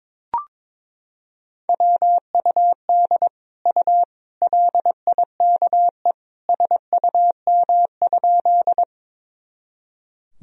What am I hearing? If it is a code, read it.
Morse code, heard as a keyed tone, about 22 wpm, 700 Hz. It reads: WUD U LIKE SUM?
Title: Sentences from Top 100 Words and Abbreviations 22wpm